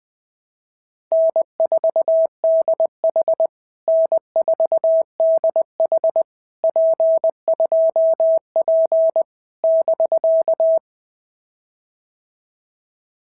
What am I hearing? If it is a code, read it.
Morse code, heard as a keyed tone, about 19 wpm, 650 Hz. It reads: N4DH N4DH P2P <BK>
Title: POTA QSO Practice - 20wpm